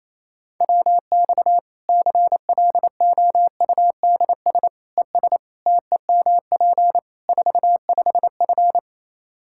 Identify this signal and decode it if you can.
Morse code, heard as a keyed tone, about 28 wpm, 700 Hz. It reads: WX CLOUDS ES TEMP 45F
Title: Practice QSO - Exchange 2 - Info - 28wpm